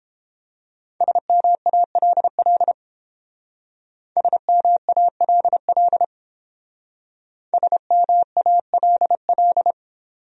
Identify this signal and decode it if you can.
Morse code, heard as a keyed tone, about 31 wpm, 700 Hz. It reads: SMALL SMALL SMALL